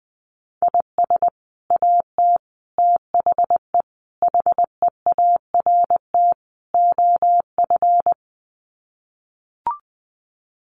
Morse code, heard as a keyed tone, about 20 wpm, 700 Hz. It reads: IS AT THE HEART OF
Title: Sets of 5 Words Encore2 20wpm